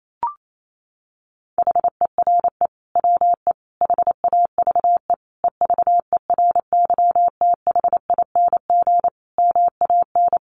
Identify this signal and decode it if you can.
Morse code, heard as a keyed tone, about 28 wpm, 700 Hz. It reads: HERE WE HAVE EVERYTHING MAN
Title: Sentences from Top 1100 Words 28wpm